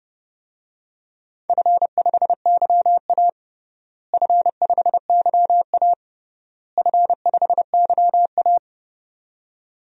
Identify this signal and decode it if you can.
Morse code, heard as a keyed tone, about 30 wpm, 700 Hz. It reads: F5YA F5YA F5YA
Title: DXCC-100 Call Signs - 4 Characters - Repeated 3x - 30wpm